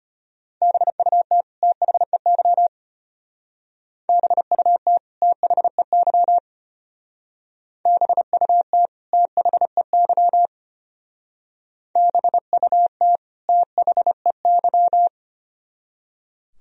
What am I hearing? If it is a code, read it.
Morse code, heard as a keyed tone, about 30 wpm, 700 Hz. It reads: BUT THEY BUT THEY BUT THEY BUT THEY
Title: Speed-Racing - Sets of 2 Words 25wpm